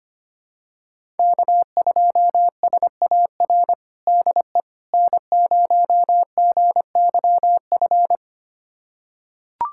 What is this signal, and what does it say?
Morse code, heard as a keyed tone, about 25 wpm, 700 Hz. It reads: K2SAR DE N0GYF
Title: USA Callsign DE Callsign - 5 Characters - 25wpm